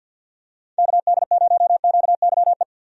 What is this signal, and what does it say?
Morse code, heard as a keyed tone, about 49 wpm, 700 Hz. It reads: KD0QYE
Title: USA Call Signs - 6 Characters - 50wpm